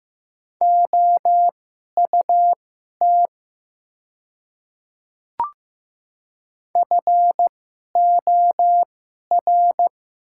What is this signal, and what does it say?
Morse code, heard as a keyed tone, about 15 wpm, spaced as for 11 wpm, 700 Hz. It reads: OUT FOR
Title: 2x Character Spacing - Top 100 Words - 15wpm